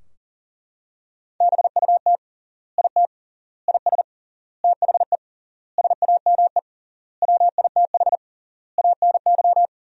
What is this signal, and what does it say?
Morse code, heard as a keyed tone, about 40 wpm, 700 Hz. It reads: BUT IT IS THE SAME WITH ANY
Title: 3x Word-Spacing - Sentences from Top 300 Words - 40wpm